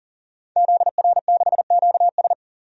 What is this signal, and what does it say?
Morse code, heard as a keyed tone, about 40 wpm, 700 Hz. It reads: ZR6QS
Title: DXCC-100 Call Signs - 5 Characters - 40wpm